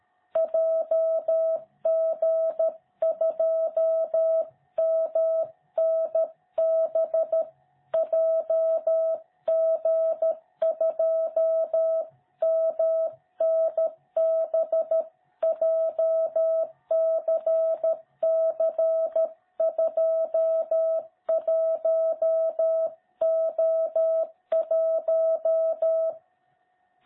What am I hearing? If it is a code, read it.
Morse code, heard as a keyed tone, about 13 wpm, 660 Hz. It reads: JG2MNB JG2MNB JCC21O1
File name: jh8cbh / 2021cw